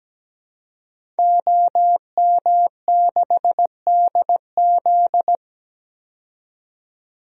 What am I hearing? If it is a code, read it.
Morse code, heard as a keyed tone, about 17 wpm, 700 Hz. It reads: OM6DZ
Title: DXCC-100 Call Signs - 5 Characters - 17wpm